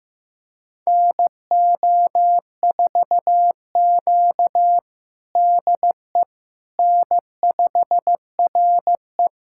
Morse code, heard as a keyed tone, about 16 wpm, 700 Hz. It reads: NO4Q DE N5RE
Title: USA Callsign DE Callsign - 4 Characters - 15wpm